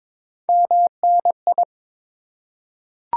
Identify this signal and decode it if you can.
Morse code, heard as a keyed tone, about 21 wpm, 690 Hz. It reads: MNI
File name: Top 90 QSO Elements - 3x Character Spacing - 22wpm